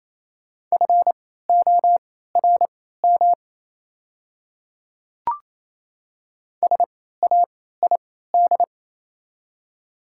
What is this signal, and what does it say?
Morse code, heard as a keyed tone, about 28 wpm, spaced as for 16 wpm, 700 Hz. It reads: FORM SAID
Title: 3x Character Spacing - Top 200 Words - 28wpm